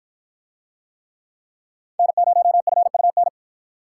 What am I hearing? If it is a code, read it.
Morse code, heard as a keyed tone, about 53 wpm, 700 Hz. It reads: N0FUN